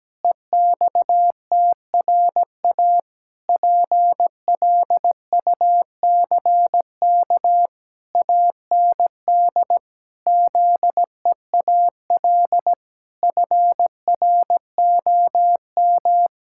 Morse code, heard as a keyed tone, about 17 wpm, 700 Hz. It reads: EXTRA PLUCK AND ZEAL FROM
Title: Pangram Challenge v2 - 17wpm